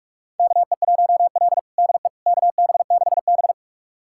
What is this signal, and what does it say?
Morse code, heard as a keyed tone, about 46 wpm, 700 Hz. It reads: KE1L DE KB6B
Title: USA Callsign DE Callsign - 4 Characters - 45wpm